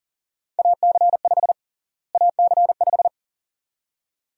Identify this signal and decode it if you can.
Morse code, heard as a keyed tone, about 37 wpm, 700 Hz. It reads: AC5 AC5
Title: USA Call Sign Prefixes - Repeated 2x - 40wpm